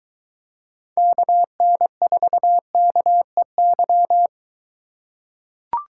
Morse code, heard as a keyed tone, about 21 wpm, 700 Hz. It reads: KN4KEY